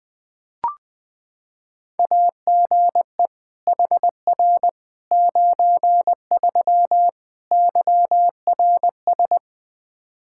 Morse code, heard as a keyed tone, about 20 wpm, 700 Hz. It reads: AGE HR 93 YRS
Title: Practice QSO - Exchange 3 - More Info - 20wpm